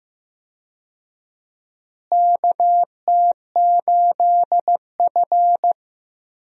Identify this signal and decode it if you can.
Morse code, heard as a keyed tone, about 15 wpm, 700 Hz. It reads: KT8F
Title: USA Call Signs - 4 Characters - Repeated 3x - 15wpm